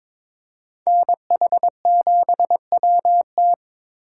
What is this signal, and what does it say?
Morse code, heard as a keyed tone, about 26 wpm, 700 Hz. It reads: NH7WT
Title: Speed Racing - USA 2x2 Call Signs 22wpm